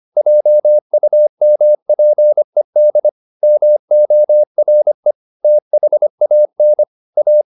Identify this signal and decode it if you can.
Morse code, heard as a keyed tone, about 25 wpm, 600 Hz. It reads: JUMPED MORE THAN A